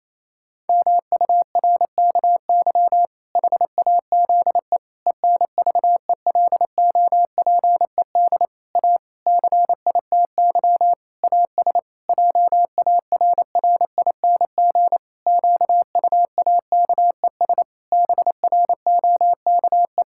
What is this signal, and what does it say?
Morse code, heard as a keyed tone, about 28 wpm, 700 Hz. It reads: MURKY HAZE ENVELOPED A CITY AS JARRING QUAKES BROKE
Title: Pangram Challenge v2 - 28wpm